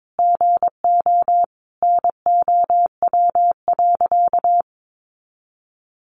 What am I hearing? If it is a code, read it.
Morse code, heard as a keyed tone, about 22 wpm, 700 Hz. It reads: GO NOW.
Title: Sentences from Top 1300 Words 22wpm